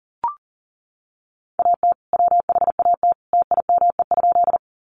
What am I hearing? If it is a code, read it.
Morse code, heard as a keyed tone, about 40 wpm, 700 Hz. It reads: AT WHAT TIME?
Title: Sentences from Top 100 Words 40wpm